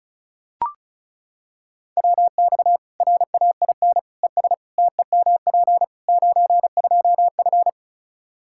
Morse code, heard as a keyed tone, about 34 wpm, 700 Hz. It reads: WX RAIN ES TEMP 92F
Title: Practice QSO - Exchange 2 - Info - 35wpm